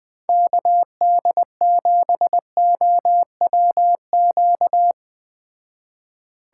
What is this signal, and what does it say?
Morse code, heard as a keyed tone, about 20 wpm, 700 Hz. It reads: KD7OWQ
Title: Speed Racing - USA 2x3 Call Signs 17wpm